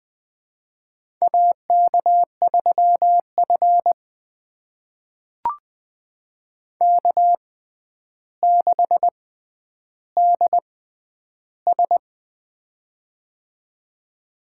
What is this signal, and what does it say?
Morse code, heard as a keyed tone, about 20 wpm, spaced as for 10 wpm, 700 Hz. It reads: AK3F K6DS
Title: USA Call Signs - 4 Characters - 6x Character Spacing - 20wpm